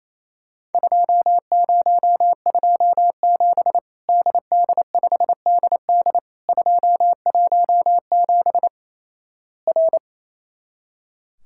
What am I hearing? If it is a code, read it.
Morse code, heard as a keyed tone, about 28 wpm, 700 Hz. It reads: 2027 DD5DD 217 R
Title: Worked All Europe DX Contest - QTC Exchange - 28wpm